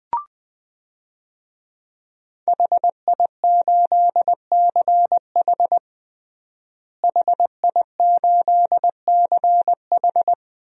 Morse code, heard as a keyed tone, about 20 wpm, 700 Hz. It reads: HI8CH HI8CH
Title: DXCC-100 Call Signs - 5 Characters - Repeated 3x - 20wpm